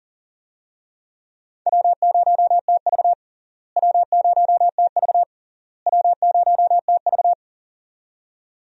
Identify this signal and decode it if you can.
Morse code, heard as a keyed tone, about 40 wpm, 700 Hz. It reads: W0TV W0TV W0TV
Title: USA Call Signs - 4 Characters - Repeated 3x - 40wpm